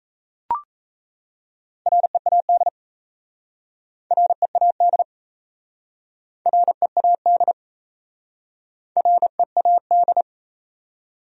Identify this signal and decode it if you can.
Morse code, heard as a keyed tone, about 38 wpm, 700 Hz. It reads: READ READ READ READ